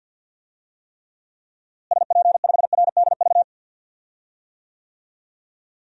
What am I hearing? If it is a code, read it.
Morse code, heard as a keyed tone, about 50 wpm, 700 Hz. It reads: SP5RDV
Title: DXCC-100 Call Signs - 6 Characters - Repeated 3x - 50wpm